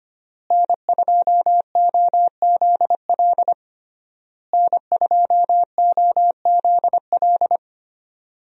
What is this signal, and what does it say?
Morse code, heard as a keyed tone, about 25 wpm, 700 Hz. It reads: N2OZL N2OZL
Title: USA Call Signs - 5 Characters - Repeated 3x - 25wpm